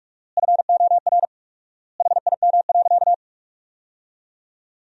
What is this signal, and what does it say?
Morse code, heard as a keyed tone, about 41 wpm, 690 Hz. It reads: FOR HIM.
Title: 4x Word-Spacing - Sentences from Top 500 Words - 45wpm